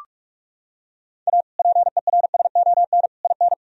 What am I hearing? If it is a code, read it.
Morse code, heard as a keyed tone, about 46 wpm, 700 Hz. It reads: A PERSON IN
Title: Sentences from Top 1000 Words - Review Part 4 - 45wpm